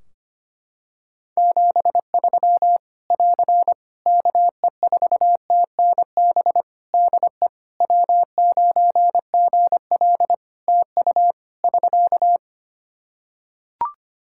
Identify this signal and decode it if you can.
Morse code, heard as a keyed tone, about 25 wpm, 700 Hz. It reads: 73 <AR> KE4TNB DE W9GL TU <SK>
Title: K1USN SST QSO Practice 25wpm